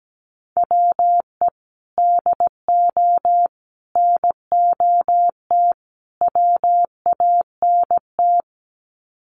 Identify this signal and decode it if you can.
Morse code, heard as a keyed tone, about 17 wpm, 700 Hz. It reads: WE DO NOT WANT
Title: Sets of 4 Words 17wpm